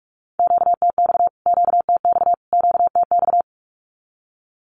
Morse code, heard as a keyed tone, about 44 wpm, 700 Hz. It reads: QTX QTX QTX
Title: Q-Codes Repeated 3x 45wpm